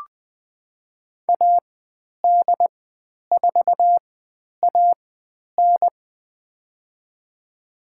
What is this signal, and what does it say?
Morse code, heard as a keyed tone, about 20 wpm, spaced as for 10 wpm, 700 Hz. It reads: AD4AN
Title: 102 Lesson 17g - Letter u - Call Signs 20wpm